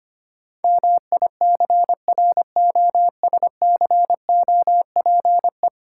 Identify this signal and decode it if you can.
Morse code, heard as a keyed tone, about 25 wpm, 700 Hz. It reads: MICROSCOPE
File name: practice / 25wpm / Easy Words with Hints - 25wpm